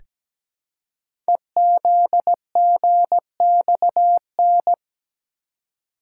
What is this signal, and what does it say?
Morse code, heard as a keyed tone, about 17 wpm, 700 Hz. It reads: EZGXN
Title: Code Groups - 5 Letter:Numbers - 17wpm